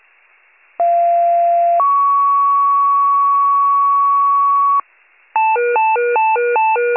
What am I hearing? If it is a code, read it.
Two-tone page, heard as a one-second tone, then a three-second tone, about 690 then 1120 Hz, with noise. It is heard from 0.8 to 4.8 s.